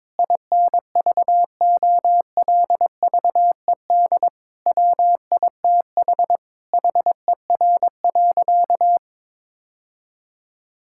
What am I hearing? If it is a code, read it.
Morse code, heard as a keyed tone, about 21 wpm, 700 Hz. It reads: INVOLVED WITH HER.
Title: Sentences from Top 1000 Words - Review Part 5 - 22wpm